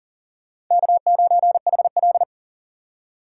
Morse code, heard as a keyed tone, about 40 wpm, 700 Hz. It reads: K9HL
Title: USA Call Signs - 2x Character Spacing - 40wpm